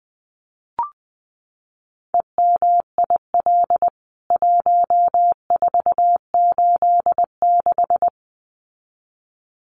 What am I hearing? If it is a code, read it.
Morse code, heard as a keyed tone, about 20 wpm, 700 Hz. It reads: EMIL 1486
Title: CWT Exchange 20wpm